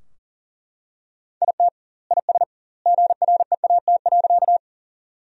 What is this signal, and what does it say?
Morse code, heard as a keyed tone, about 40 wpm, 700 Hz. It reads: IT IS GREAT.
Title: 2x Word-Spacing - Sentences from Top 200 Words - 40wpm